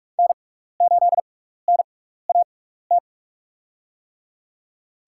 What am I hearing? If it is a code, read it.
Morse code, heard as a keyed tone, about 46 wpm, spaced as for 15 wpm, 700 Hz. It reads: N8NAT
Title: USA Call Signs - 5 Characters - 6x Character Spacing - 45wpm